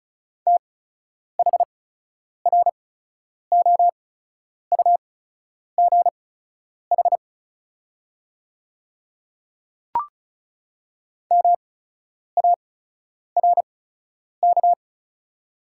Morse code, heard as a keyed tone, about 36 wpm, spaced as for 10 wpm, 700 Hz. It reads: THROUGH MARK
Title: 8x Character Spacing - Top 300 Words - 35wpm